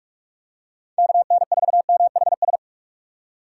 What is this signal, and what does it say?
Morse code, heard as a keyed tone, about 46 wpm, 700 Hz. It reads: KN4MHS
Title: USA Call Signs - 6 Characters - 6x Character Spacing - 45wpm